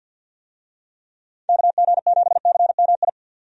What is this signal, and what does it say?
Morse code, heard as a keyed tone, about 47 wpm, 700 Hz. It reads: KG7CMI
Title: USA Call Signs - 6 Characters - 50wpm